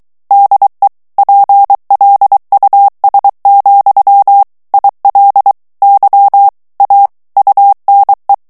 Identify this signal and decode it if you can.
Morse code, heard as a keyed tone, about 23 wpm, 800 Hz. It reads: DE PLUS, IL Y A UNE
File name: morse mp3 / FRANCAIS TEXTE CLAIR 1 VITESSE 22 MOTS MINUTES